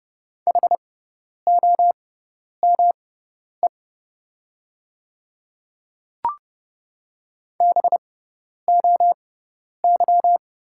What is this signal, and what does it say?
Morse code, heard as a keyed tone, about 30 wpm, spaced as for 10 wpm, 700 Hz. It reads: HOME BOY
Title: 6x Character Spacing - Top 200 Words - 30wpm